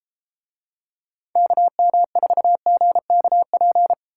Morse code, heard as a keyed tone, about 32 wpm, 690 Hz. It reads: KM4GKP